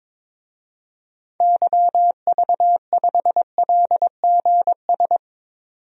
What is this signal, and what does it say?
Morse code, heard as a keyed tone, about 22 wpm, 700 Hz. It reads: YV5LGS
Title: DXCC-100 Call Signs - 6 Characters - 22wpm